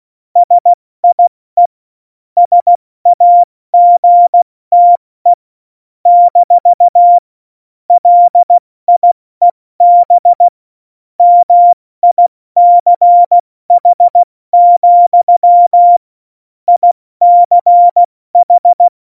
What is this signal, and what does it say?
Morse code, heard as a keyed tone, about 16 wpm, 700 Hz. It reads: SIE SAGTE - LIEB MICH, ICH
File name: Morsekurs / Witze / Witz 29. Dezember 2020, 70 F80